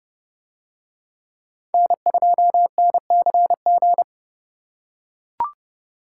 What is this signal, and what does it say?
Morse code, heard as a keyed tone, about 30 wpm, 700 Hz. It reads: N2NCG